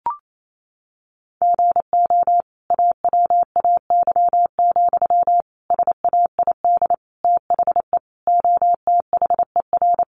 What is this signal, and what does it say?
Morse code, heard as a keyed tone, about 28 wpm, 700 Hz. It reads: GO AWAY, SAID THE OTHER
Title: Sentences from Top 1500 Words 28wpm